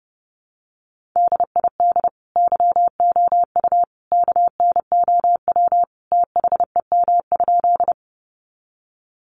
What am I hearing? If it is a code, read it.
Morse code, heard as a keyed tone, about 30 wpm, 700 Hz. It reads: DID YOU KNOW THEM?
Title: Sentences from Top 100 Words 30wpm